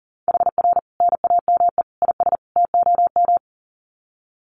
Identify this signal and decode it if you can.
Morse code, heard as a keyed tone, about 40 wpm, 700 Hz. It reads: HR NAME IS TOM
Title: Common QSO Phrases 40wpm